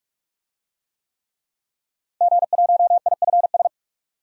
Morse code, heard as a keyed tone, about 46 wpm, 700 Hz. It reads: G1IFS
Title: DXCC-100 Call Signs - 5 Characters - Repeated 3x - 45wpm